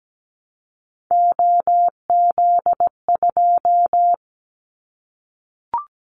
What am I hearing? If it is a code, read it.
Morse code, heard as a keyed tone, about 17 wpm, 700 Hz. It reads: OZ2
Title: Callsign Prefixes for Top 50 DXCC - 17wpm